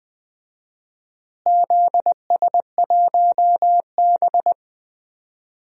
Morse code, heard as a keyed tone, about 20 wpm, 700 Hz. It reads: ZS1B
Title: DXCC-100 Call Signs - 4 Characters - Repeated 2x - 20wpm